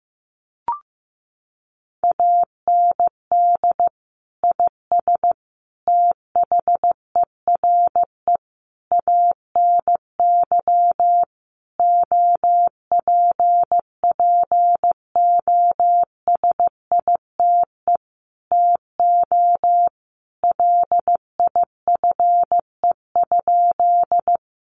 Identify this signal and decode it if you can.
Morse code, heard as a keyed tone, about 15 wpm, 700 Hz. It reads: AND IS THERE ANY OPPOSITE TO LIFE?
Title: Sentences from Top 1300 Words 15wpm